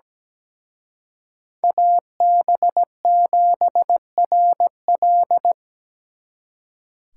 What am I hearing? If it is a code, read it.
Morse code, heard as a keyed tone, about 20 wpm, 700 Hz. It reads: AB7RL
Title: Speed Racing - USA Call Signs 17wpm